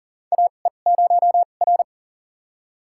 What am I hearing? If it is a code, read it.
Morse code, heard as a keyed tone, about 40 wpm, spaced as for 29 wpm, 700 Hz. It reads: AE0R
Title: USA Call Signs - 4 Characters - 2x Character Spacing - 40wpm